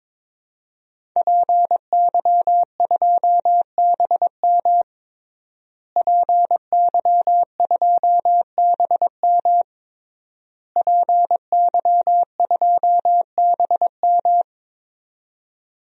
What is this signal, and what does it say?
Morse code, heard as a keyed tone, about 22 wpm, 700 Hz. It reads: PY2BM PY2BM PY2BM
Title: DXCC-100 Call Signs - 5 Characters - Repeated 3x - 22wpm